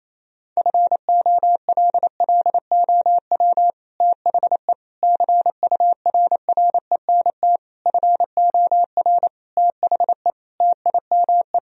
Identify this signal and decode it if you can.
Morse code, heard as a keyed tone, about 28 wpm, 700 Hz. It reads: FOLLOW THE CURRENT FOR THE TIME
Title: Sentences from Top 1000 Words - Review Part 4 - 28wpm